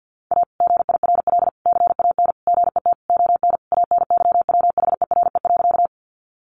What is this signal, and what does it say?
Morse code, heard as a keyed tone, about 50 wpm, 700 Hz. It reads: A GIRL CAN GET ON ANYWHERE.
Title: Sentences from Top 1400 Words 50wpm